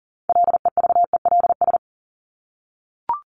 Morse code, heard as a keyed tone, about 40 wpm, 700 Hz. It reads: LEVELS
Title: ES:S Suffix - Top 100 Words with Base Words 40wpm